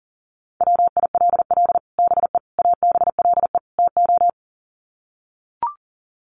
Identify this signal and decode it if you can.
Morse code, heard as a keyed tone, about 37 wpm, 700 Hz. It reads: WILL BE ABLE TO
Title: Sets of 4 Words 40wpm